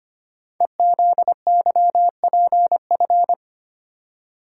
Morse code, heard as a keyed tone, about 25 wpm, 700 Hz. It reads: EZYPF
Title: Code Groups - 5 Letters - 25wpm